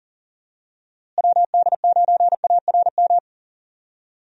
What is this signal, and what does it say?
Morse code, heard as a keyed tone, about 40 wpm, 700 Hz. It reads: WD9ARM
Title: USA Call Signs - 6 Characters - Repeated 2x - 40wpm